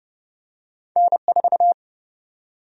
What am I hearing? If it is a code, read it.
Morse code, heard as a keyed tone, about 30 wpm, 700 Hz. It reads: N4
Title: USA Call Sign Prefixes - 30wpm